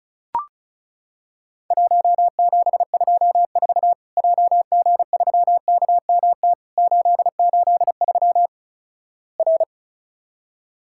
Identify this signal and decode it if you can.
Morse code, heard as a keyed tone, about 36 wpm, 700 Hz. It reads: 1724 JG3KMT 883 R
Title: Worked All Europe DX Contest - QTC Exchange - 35wpm